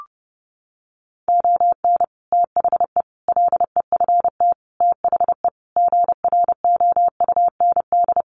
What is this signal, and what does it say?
Morse code, heard as a keyed tone, about 30 wpm, 700 Hz. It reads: ON THE LEFT THE GROUND
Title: Sentences from Top 700 Words 30wpm